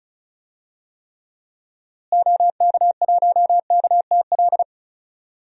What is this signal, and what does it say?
Morse code, heard as a keyed tone, about 36 wpm, 700 Hz. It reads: OK1KTL
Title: DXCC-100 Call Signs - 6 Characters - Repeated 2x - 35wpm